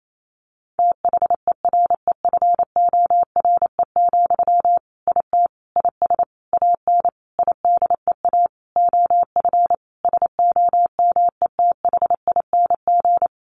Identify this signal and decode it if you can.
Morse code, heard as a keyed tone, about 28 wpm, 700 Hz. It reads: THEREFORE, IT IS AN IDEA OF SOMETHING
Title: Sentences from Top 1500 Words 28wpm